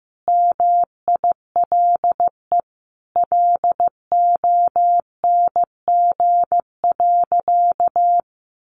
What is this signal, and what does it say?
Morse code, heard as a keyed tone, about 15 wpm, 700 Hz. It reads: MILE LONG.
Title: Sentences from Top 1300 Words 15wpm